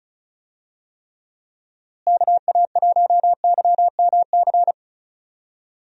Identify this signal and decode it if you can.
Morse code, heard as a keyed tone, about 36 wpm, 700 Hz. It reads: KA1YMC
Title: USA Call Signs - 6 Characters - Repeated 3x - 35wpm